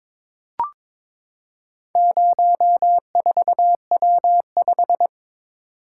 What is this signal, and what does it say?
Morse code, heard as a keyed tone, about 22 wpm, 700 Hz. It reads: 04W5
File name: practice / 22wpm / Four Letter-Number Combinations - 22wpm